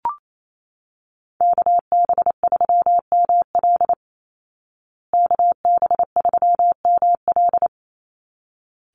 Morse code, heard as a keyed tone, about 28 wpm, 700 Hz. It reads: KB3ML KB3ML
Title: Call Signs Repeated 3x - 28wpm